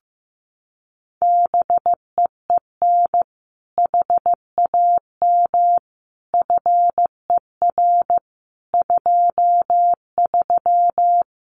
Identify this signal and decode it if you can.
Morse code, heard as a keyed tone, about 15 wpm, 700 Hz. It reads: BEEN HAM FER 23
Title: Common QSO Phrases Encore 15wpm